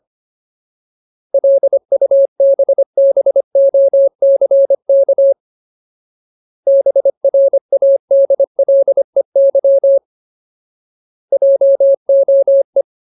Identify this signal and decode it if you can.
Morse code, heard as a keyed tone, about 25 wpm, 550 Hz. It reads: LUBBOCK BRADLEY JOE
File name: Ultimate Word List / 25WPM Ultimate